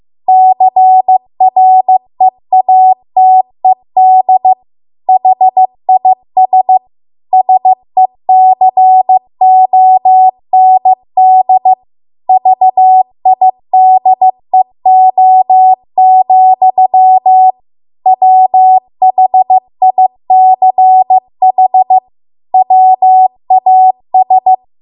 Morse code, heard as a keyed tone, about 15 wpm, 750 Hz. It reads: CREATED HIS SECOND VIDEO, WHICH WAS